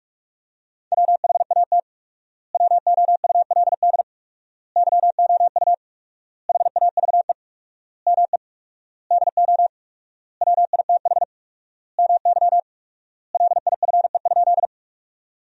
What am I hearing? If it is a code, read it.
Morse code, heard as a keyed tone, about 42 wpm, 700 Hz. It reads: WHAT WOULD YOU HAVE ME DO WITH MY LIFE?
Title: 4x Word-Spacing - Sentences from Top 300 Words - 45wpm